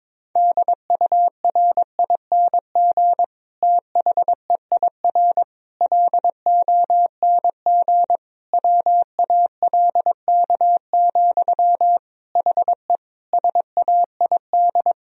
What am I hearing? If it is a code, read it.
Morse code, heard as a keyed tone, about 22 wpm, 700 Hz. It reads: DURING THEIR LONG WALK, HE SAID
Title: Sentences from Top 1000 Words - Review Part 2 - 22wpm